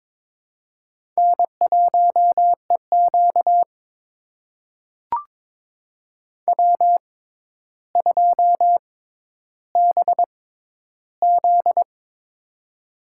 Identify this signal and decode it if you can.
Morse code, heard as a keyed tone, about 22 wpm, spaced as for 11 wpm, 700 Hz. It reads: N1EQ W2BZ
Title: USA Call Signs - 4 Characters - 6x Character Spacing - 22wpm